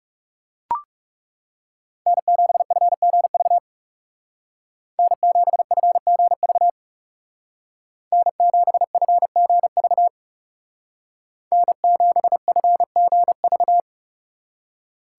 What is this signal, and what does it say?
Morse code, heard as a keyed tone, about 39 wpm, 700 Hz. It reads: N7FGV N7FGV N7FGV N7FGV